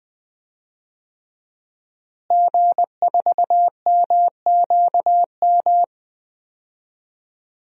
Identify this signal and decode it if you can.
Morse code, heard as a keyed tone, about 20 wpm, 700 Hz. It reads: G4MQM
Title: DXCC-100 Call Signs - 5 Characters - Repeated 3x - 20wpm